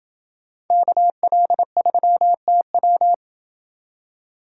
Morse code, heard as a keyed tone, about 28 wpm, 700 Hz. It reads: KL3TW